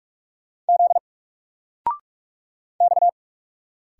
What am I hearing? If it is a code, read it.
Morse code, heard as a keyed tone, about 44 wpm, 700 Hz. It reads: Z X
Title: Single Character (Letter:Number:Punctuation:Prosign) 45wpm